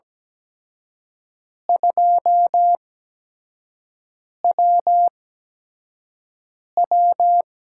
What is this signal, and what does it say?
Morse code, heard as a keyed tone, about 17 wpm, spaced as for 5 wpm, 700 Hz. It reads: 2WW
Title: USA Call Signs - 8x Character Spacing - 17wpm